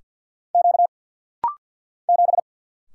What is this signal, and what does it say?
Morse code, heard as a keyed tone, about 49 wpm, 700 Hz. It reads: Q 7
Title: Single Letter-Number - Rapid-Fire 50wpm